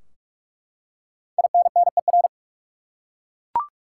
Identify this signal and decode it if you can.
Morse code, heard as a keyed tone, about 42 wpm, 700 Hz. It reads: INNER
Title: Five Letter Words - 45wpm